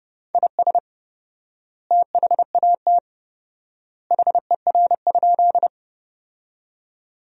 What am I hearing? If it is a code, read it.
Morse code, heard as a keyed tone, about 32 wpm, 700 Hz. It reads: IS THAT HER?
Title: 4x Word-Spacing - Sentences from Top 100 Words - 30wpm